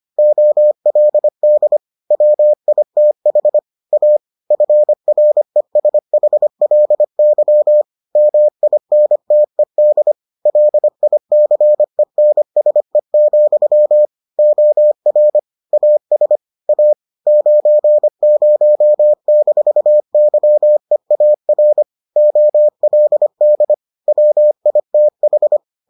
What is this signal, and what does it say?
Morse code, heard as a keyed tone, about 25 wpm, 600 Hz. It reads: OLD WITH A FRESHLY MINTED LICENSE, OR AS A 90-YEAR OLD WITH